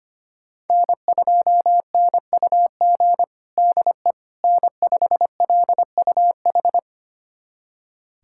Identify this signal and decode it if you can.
Morse code, heard as a keyed tone, about 25 wpm, 700 Hz. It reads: N2NUG DE N5LUH
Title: USA Callsign DE Callsign - 5 Characters - 25wpm